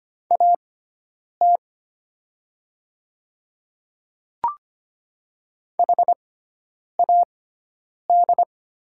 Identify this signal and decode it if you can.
Morse code, heard as a keyed tone, about 25 wpm, spaced as for 9 wpm, 700 Hz. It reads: AT HAD
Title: 6x Character Spacing - Top 300 Words - 25wpm